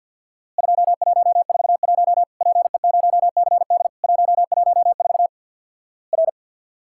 Morse code, heard as a keyed tone, about 50 wpm, 700 Hz. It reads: 2141 PE0CD 114 R
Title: Worked All Europe DX Contest - QTC Exchange with Repeats - 50wpm